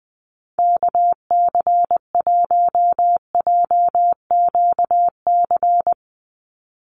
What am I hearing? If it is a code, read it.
Morse code, heard as a keyed tone, about 20 wpm, 700 Hz. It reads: KC1JQC
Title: USA Call Signs 20wpm